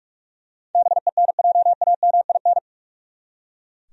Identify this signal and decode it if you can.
Morse code, heard as a keyed tone, about 44 wpm, 700 Hz. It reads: BENJAMIN
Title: USA Common Names - 45wpm